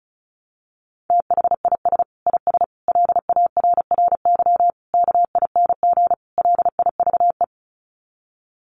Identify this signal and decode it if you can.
Morse code, heard as a keyed tone, about 34 wpm, 700 Hz. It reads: THIS IS LARRY KING LIVE
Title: Sets of 5 Words Encore2 35wpm